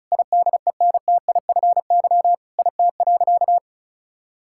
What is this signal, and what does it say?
Morse code, heard as a keyed tone, about 36 wpm, 700 Hz. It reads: IDENTIFY IT.
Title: Sentences from Top 1000 Words - Review Part 6 - 35wpm